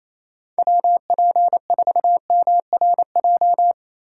Morse code, heard as a keyed tone, about 35 wpm, 700 Hz. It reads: WP4MRJ
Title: Speed Racing - USA 2x3 Call Signs 28wpm